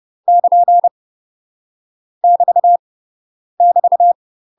Morse code, heard as a keyed tone, about 30 wpm, 700 Hz. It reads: <KN> <BT> <BT>
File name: UK MASTER QSOs TRAINING / 30WPM / UK MASTER QSOs 0028